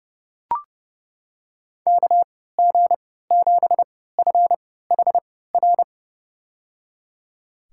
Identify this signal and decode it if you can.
Morse code, heard as a keyed tone, about 30 wpm, spaced as for 17 wpm, 700 Hz. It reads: KG7FHR
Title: USA Call Signs - 3x Character Spacing - 30wpm